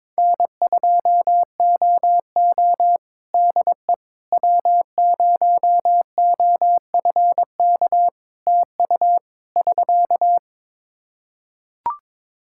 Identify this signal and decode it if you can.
Morse code, heard as a keyed tone, about 22 wpm, 700 Hz. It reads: N2OO DE W0OFK TU <SK>
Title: K1USN SST QSO Practice 22wpm